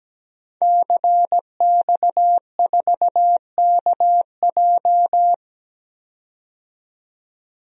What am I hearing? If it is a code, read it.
Morse code, heard as a keyed tone, about 17 wpm, 700 Hz. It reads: CX4KJ
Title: DXCC-100 Call Signs - 5 Characters - 17wpm